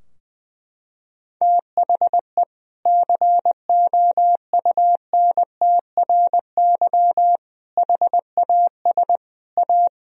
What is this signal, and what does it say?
Morse code, heard as a keyed tone, about 20 wpm, 700 Hz. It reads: THE COUNTRY HAS A
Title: Sentences from Top 1000 Words - Review Part 4 - 20wpm